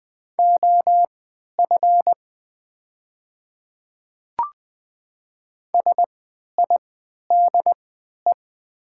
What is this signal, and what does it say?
Morse code, heard as a keyed tone, about 20 wpm, spaced as for 11 wpm, 700 Hz. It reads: OF SIDE
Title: 3x Character Spacing - Top 100 Words - 20wpm